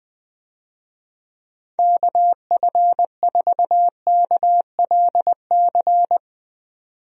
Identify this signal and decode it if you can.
Morse code, heard as a keyed tone, about 20 wpm, 700 Hz. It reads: KF4KLC
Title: USA Call Signs - 6 Characters - 20wpm